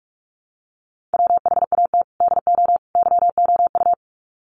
Morse code, heard as a keyed tone, about 44 wpm, 700 Hz. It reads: WHAT DO YOU
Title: Sets of 3 Words 45wpm